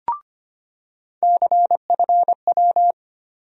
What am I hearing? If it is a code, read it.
Morse code, heard as a keyed tone, about 25 wpm, 700 Hz. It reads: CFW
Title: USA Call Sign Postfixes - 3 Letters 25wpm